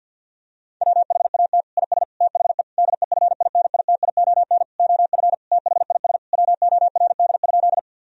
Morse code, heard as a keyed tone, about 50 wpm, 700 Hz. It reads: WHAT IS THE DEFINITION OF THIS WORD?
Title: Sentences from Top 1000 Words - Review Part 4 - 50wpm